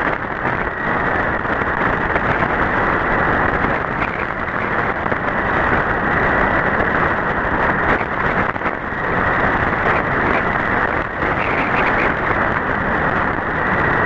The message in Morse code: none